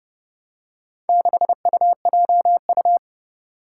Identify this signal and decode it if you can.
Morse code, heard as a keyed tone, about 31 wpm, 700 Hz. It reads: 6UJU
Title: Four Challenging Letter-Number Combinations 30wpm